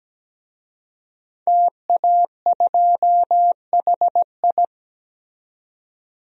Morse code, heard as a keyed tone, about 17 wpm, 700 Hz. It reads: TA2HI
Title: DXCC-100 Call Signs - 5 Characters - 17wpm